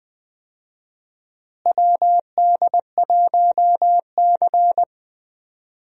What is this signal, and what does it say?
Morse code, heard as a keyed tone, about 20 wpm, 700 Hz. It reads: WD1C
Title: USA Call Signs - 4 Characters - Repeated 2x - 20wpm